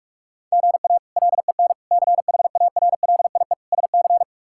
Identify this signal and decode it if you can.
Morse code, heard as a keyed tone, about 42 wpm, 700 Hz. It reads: GA LEN CHARLIE SC
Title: K1USN SST QSO Practice - 45wpm